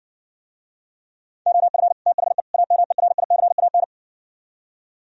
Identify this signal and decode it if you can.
Morse code, heard as a keyed tone, about 58 wpm, 700 Hz. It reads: OF THE AMERICAN